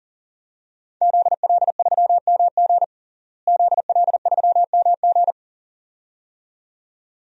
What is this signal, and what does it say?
Morse code, heard as a keyed tone, about 37 wpm, 700 Hz. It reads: ZL3MG ZL3MG
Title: DXCC-100 Call Signs - 5 Characters - Repeated 2x - 40wpm